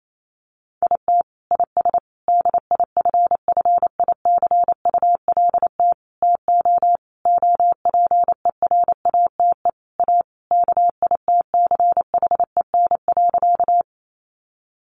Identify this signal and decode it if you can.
Morse code, heard as a keyed tone, about 28 wpm, 700 Hz. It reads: IT IS DIFFICULT TO OPERATE A KITCHEN.
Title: Sentences from Top 1500 Words 28wpm